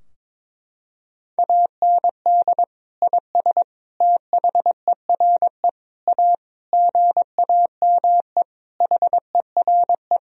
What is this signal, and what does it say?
Morse code, heard as a keyed tone, about 22 wpm, 700 Hz. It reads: AND IS THERE A GAME HERE
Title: Sentences from Top 1000 Words - Review Part 3 - 22wpm